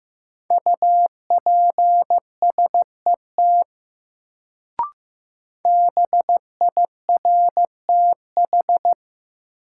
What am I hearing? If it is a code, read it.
Morse code, heard as a keyed tone, about 15 wpm, 700 Hz. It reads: UPSET BIRTH
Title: Sentences from Top 800 Words - New Word Drill - 15wpm